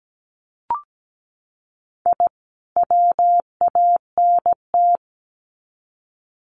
Morse code, heard as a keyed tone, about 17 wpm, 700 Hz. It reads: I WANT
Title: Sets of 2 Words 17wpm